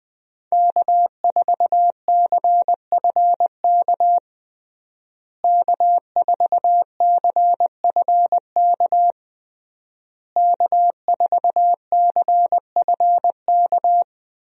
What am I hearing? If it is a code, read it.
Morse code, heard as a keyed tone, about 21 wpm, 700 Hz. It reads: K4CFK K4CFK K4CFK
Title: USA Call Signs - 5 Characters - Repeated 3x - 20wpm